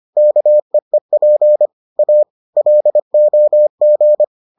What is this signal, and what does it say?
Morse code, heard as a keyed tone, about 25 wpm, 600 Hz. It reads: KEEP A LOG